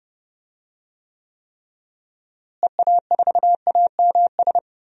Morse code, heard as a keyed tone, about 32 wpm, 700 Hz. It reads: EA4AMS
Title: DXCC-100 Call Signs - 6 Characters - Repeated 3x - 30wpm